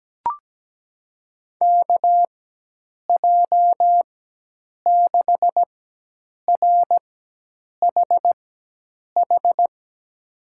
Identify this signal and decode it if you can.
Morse code, heard as a keyed tone, about 17 wpm, spaced as for 8 wpm, 700 Hz. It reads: KJ6RHH
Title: USA Call Signs - 6 Characters - 4x Character Spacing - 17wpm